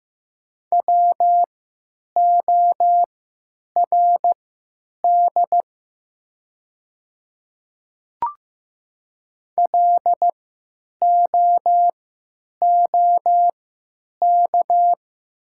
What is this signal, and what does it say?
Morse code, heard as a keyed tone, about 15 wpm, spaced as for 8 wpm, 700 Hz. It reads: WORD LOOK
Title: 3x Character Spacing - Top 100 Words - 15wpm